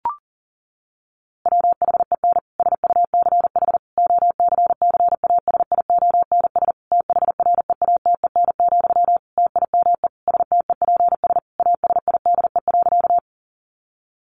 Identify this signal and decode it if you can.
Morse code, heard as a keyed tone, about 40 wpm, 700 Hz. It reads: WHEN SUCH OCCASIONS THREATEN, TIME STEPS ASIDE.
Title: A Sound Of Thunder 40wpm